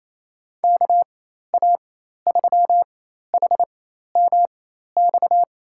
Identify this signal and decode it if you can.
Morse code, heard as a keyed tone, about 28 wpm, spaced as for 13 wpm, 700 Hz. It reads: KA3HMX
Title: USA Call Signs - 4x Character Spacing - 28wpm